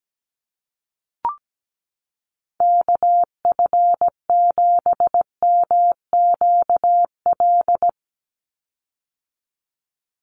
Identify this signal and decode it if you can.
Morse code, heard as a keyed tone, about 17 wpm, 700 Hz. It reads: KF7MQL
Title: USA Call Signs 17wpm